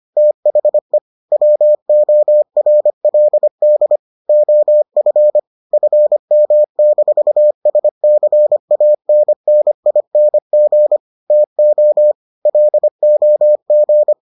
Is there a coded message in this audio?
Yes: Morse code from 0.2 to 14.1 s, heard as a keyed tone, about 25 wpm, 600 Hz. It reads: THE WORLD OF FM-SCANNING TO LOG